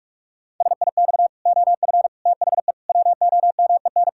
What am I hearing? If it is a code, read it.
Morse code, heard as a keyed tone, about 42 wpm, 700 Hz. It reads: SIX OF THE WOMEN
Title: Pangram Challenge v2 - 45wpm